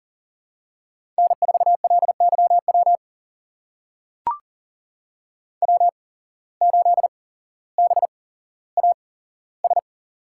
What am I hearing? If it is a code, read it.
Morse code, heard as a keyed tone, about 44 wpm, spaced as for 17 wpm, 700 Hz. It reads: N4LYW W8BAS
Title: USA Call Signs - 5 Characters - 8x Character Spacing - 40wpm